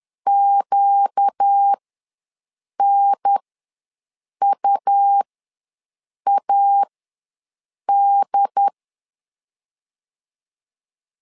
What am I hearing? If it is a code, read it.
Morse code, heard as a keyed tone, about 10 wpm, spaced as for 6 wpm, 800 Hz. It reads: QNUAD
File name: LEZIONI / 4 Q ws 10wpm 2306